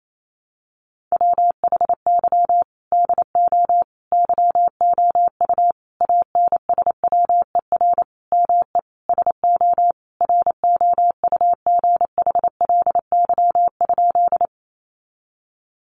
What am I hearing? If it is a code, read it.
Morse code, heard as a keyed tone, about 28 wpm, 700 Hz. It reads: WHY DO YOU ANSWER ME SO ROUGHLY?
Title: Sentences from Top 1500 Words 28wpm